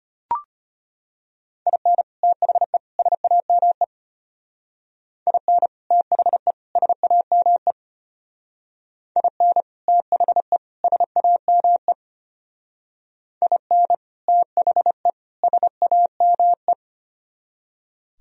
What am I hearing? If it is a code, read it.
Morse code, heard as a keyed tone, about 30 wpm, 700 Hz. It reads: IN THE SAME IN THE SAME IN THE SAME IN THE SAME
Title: Speed-Racing - Sets of 3 Words 25wpm